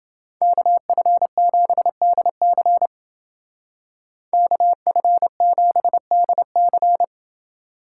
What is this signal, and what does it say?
Morse code, heard as a keyed tone, about 28 wpm, 700 Hz. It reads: KF7DC KF7DC